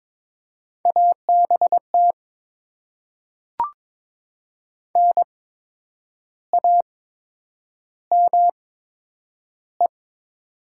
Morse code, heard as a keyed tone, about 22 wpm, spaced as for 8 wpm, 700 Hz. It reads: ABT NAME
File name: Top 90 QSO Elements - 8x Character Spacing - 22wpm